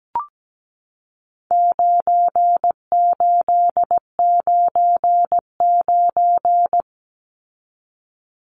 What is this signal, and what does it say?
Morse code, heard as a keyed tone, about 17 wpm, 700 Hz. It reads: 9899
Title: Numbers - 4-Digit 17wpm